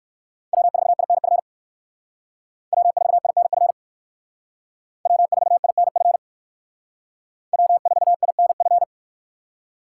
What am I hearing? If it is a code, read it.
Morse code, heard as a keyed tone, about 54 wpm, 700 Hz. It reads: W4INF W4INF W4INF W4INF
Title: Speed Racing - USA 1x3 Call Signs 45wpm